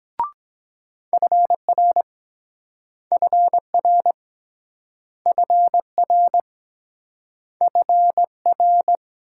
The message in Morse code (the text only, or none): FR FR FR FR